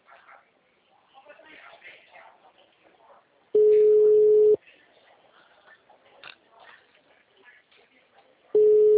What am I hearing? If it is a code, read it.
Ringback tone (Europe, 425 Hz), with speech.
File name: out- -2005